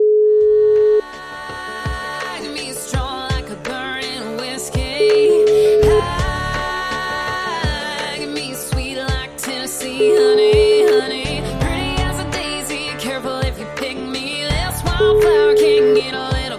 Ringback tone (Europe, 425 Hz), with noise.